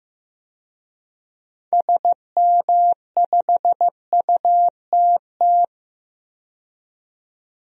Morse code, heard as a keyed tone, about 15 wpm, 700 Hz. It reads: SM5UTT
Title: DXCC-100 Call Signs - 6 Characters - 15wpm